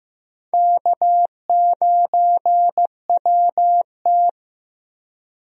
Morse code, heard as a keyed tone, about 15 wpm, 700 Hz. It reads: K9WT
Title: USA Call Signs - 4 Characters - 15wpm